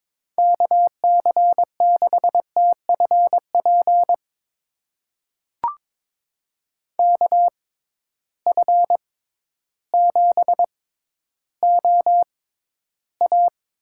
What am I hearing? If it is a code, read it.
Morse code, heard as a keyed tone, about 22 wpm, spaced as for 12 wpm, 700 Hz. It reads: KC6TFP KF7OA
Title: USA Call Signs - 6 Characters - 6x Character Spacing - 22wpm